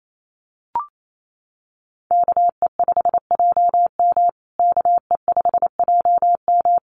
Morse code, heard as a keyed tone, about 28 wpm, 700 Hz. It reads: KE5JM KE5JM
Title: USA Call Signs Repeated 2x 28wpm